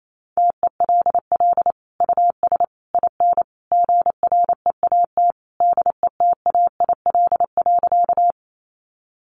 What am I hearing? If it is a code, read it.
Morse code, heard as a keyed tone, about 28 wpm, 700 Hz. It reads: TELL US IN GREAT DETAIL.
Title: Sentences from Top 1500 Words 28wpm